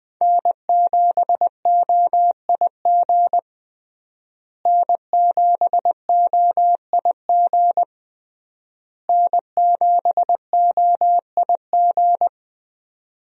Morse code, heard as a keyed tone, about 20 wpm, 700 Hz. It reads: N7OIG N7OIG N7OIG
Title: USA Call Signs - 5 Characters - Repeated 3x - 20wpm